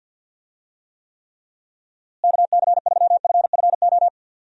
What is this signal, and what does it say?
Morse code, heard as a keyed tone, about 50 wpm, 700 Hz. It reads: KC3FFO
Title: USA Call Signs - 6 Characters - Repeated 3x - 50wpm